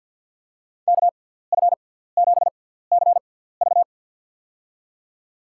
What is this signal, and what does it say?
Morse code, heard as a keyed tone, about 51 wpm, spaced as for 17 wpm, 700 Hz. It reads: KF7CV
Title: USA Call Signs - 5 Characters - 6x Character Spacing - 50wpm